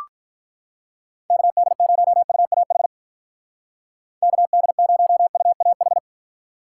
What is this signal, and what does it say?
Morse code, heard as a keyed tone, about 43 wpm, 700 Hz. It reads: KD0UAH KD0UAH
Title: Speed Racing - USA Call Signs 35wpm